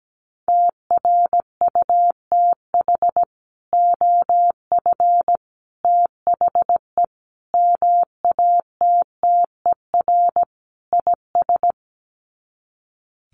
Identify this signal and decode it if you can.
Morse code, heard as a keyed tone, about 17 wpm, 700 Hz. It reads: TRUTH OF THE MATTER IS
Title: Sets of 5 Words Encore2 17wpm